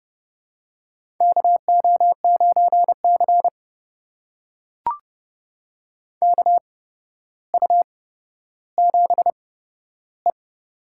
Morse code, heard as a keyed tone, about 30 wpm, spaced as for 13 wpm, 700 Hz. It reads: KO9C KU7E
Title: USA Call Signs - 4 Characters - 8x Character Spacing - 30wpm